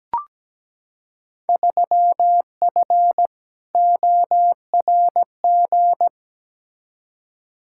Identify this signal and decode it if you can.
Morse code, heard as a keyed tone, about 17 wpm, 700 Hz. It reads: 3F ORG
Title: Field Day Exchanges - 17wpm